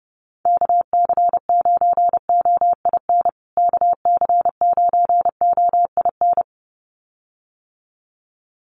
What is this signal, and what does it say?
Morse code, heard as a keyed tone, about 30 wpm, 700 Hz. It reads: KC9OIN KC9OIN
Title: USA Call Signs Repeated 2x 30wpm